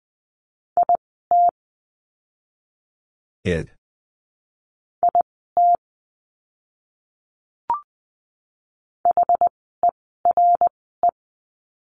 Morse code, heard as a keyed tone, about 20 wpm, spaced as for 14 wpm, 700 Hz. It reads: IT IT HERE
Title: 139 Lesson 23f - Letter f - Words 20wpm